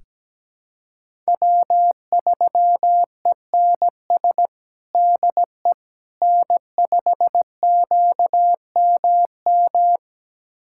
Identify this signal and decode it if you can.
Morse code, heard as a keyed tone, about 17 wpm, 700 Hz. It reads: W3ENS DE N5QMM
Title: USA Callsign DE Callsign - 5 Characters - 17wpm